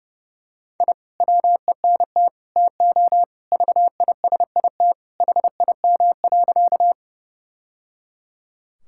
Morse code, heard as a keyed tone, about 30 wpm, 700 Hz. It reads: I WENT TO VISIT HIM.
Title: Sentences from Top 1000 Words - Review Part 6 - 30wpm